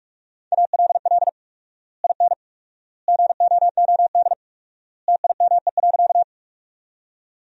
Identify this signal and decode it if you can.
Morse code, heard as a keyed tone, about 41 wpm, 700 Hz. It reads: ALL IN GOOD TIME.
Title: 4x Word-Spacing - Sentences from Top 200 Words - 45wpm